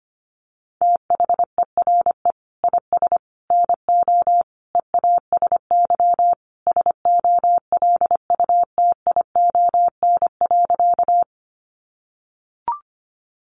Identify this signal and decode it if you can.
Morse code, heard as a keyed tone, about 25 wpm, 700 Hz. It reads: THERE IS NO EASY SOLUTION.
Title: Sentences from Top 1300 Words 25wpm